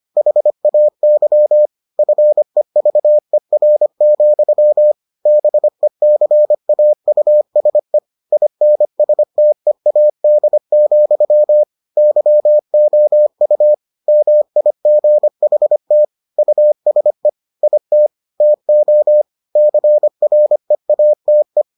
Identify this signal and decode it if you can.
Morse code, heard as a keyed tone, about 25 wpm, 600 Hz. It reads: HAY FEVER, BECAUSE INSTEAD, YOU MIGHT USE IT TO CREATE